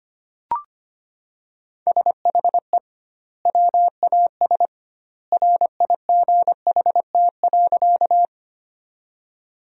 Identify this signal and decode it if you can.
Morse code, heard as a keyed tone, about 25 wpm, 700 Hz. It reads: SHE WAS RIGHT.
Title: 2x Word-Spacing - Sentences from Top 200 Words - 25wpm